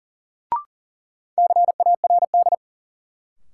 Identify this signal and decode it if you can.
Morse code, heard as a keyed tone, about 40 wpm, 700 Hz. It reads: CARD
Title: Sentences from Top 600 Words - New Word Drill - 40wpm